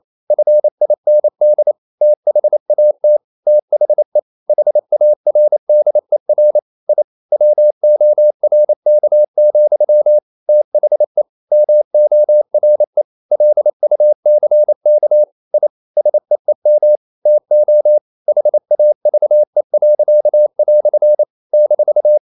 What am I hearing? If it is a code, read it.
Morse code, heard as a keyed tone, about 28 wpm, 600 Hz. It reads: FIND THAT THE HARDER I WORK, THE MORE LUCK I SEEM TO HAVE." -
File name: CW / Practice / EnglishQuotations0128